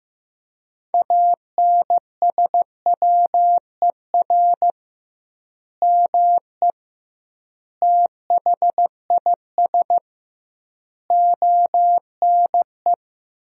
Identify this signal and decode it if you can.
Morse code, heard as a keyed tone, about 15 wpm, 700 Hz. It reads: ANSWER ME THIS ONE
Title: 2x Word-Spacing - Sentences from Top 400 Words - 15wpm